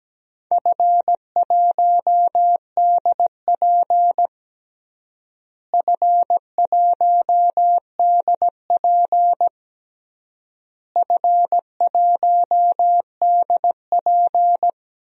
Morse code, heard as a keyed tone, about 17 wpm, 700 Hz. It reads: F1DP F1DP F1DP
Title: DXCC-100 Call Signs - 4 Characters - Repeated 3x - 17wpm